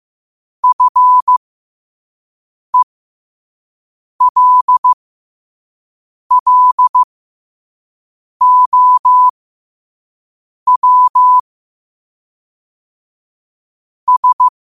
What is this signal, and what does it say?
Morse code, heard as a keyed tone, about 15 wpm, spaced as for 5 wpm, 1000 Hz. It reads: FELLOW S
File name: qotdfiles / 5wpm